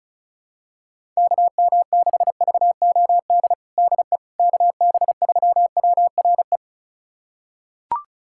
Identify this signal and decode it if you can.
Morse code, heard as a keyed tone, about 36 wpm, 700 Hz. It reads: KM6VOD DE KB3WRE
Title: USA Callsign DE Callsign 35wpm